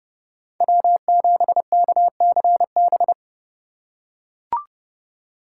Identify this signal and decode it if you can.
Morse code, heard as a keyed tone, about 30 wpm, 700 Hz. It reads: W7KCB
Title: USA Call Signs - 3x Character Spacing - 30wpm